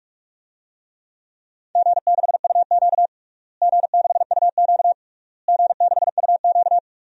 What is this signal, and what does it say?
Morse code, heard as a keyed tone, about 46 wpm, 700 Hz. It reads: G6UQ G6UQ G6UQ
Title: DXCC-100 Call Signs - 4 Characters - Repeated 3x - 45wpm